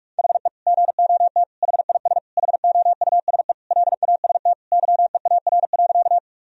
Morse code, heard as a keyed tone, about 41 wpm, 700 Hz. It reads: HE GOT HIS HOUSE LAST YEAR.
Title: Sentences from Top 1000 Words - Review Part 1 - 45wpm